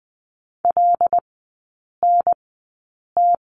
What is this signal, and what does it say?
Morse code, heard as a keyed tone, about 20 wpm, spaced as for 8 wpm, 690 Hz. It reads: LNT
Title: 087 Lesson 15e - Number 5 - Three Characters 20wpm